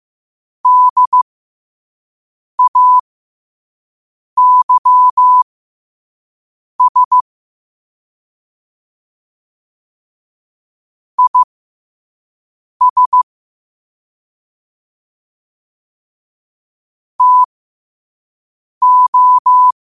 Morse code, heard as a keyed tone, about 15 wpm, spaced as for 5 wpm, 1000 Hz. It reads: DAYS IS TO
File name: qotdfiles / 5wpm